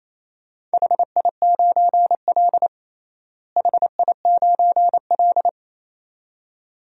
Morse code, heard as a keyed tone, about 28 wpm, 700 Hz. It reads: HI9L HI9L
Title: DXCC-100 Call Signs - 4 Characters - Repeated 2x - 28wpm